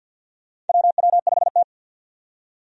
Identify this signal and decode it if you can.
Morse code, heard as a keyed tone, about 50 wpm, 700 Hz. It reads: WW5T
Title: USA Call Signs - 4 Characters - 50wpm